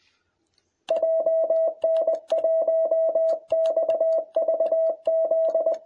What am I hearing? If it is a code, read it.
Morse code, heard as a keyed tone, about 21 wpm, 660 Hz. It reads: JN1XVZ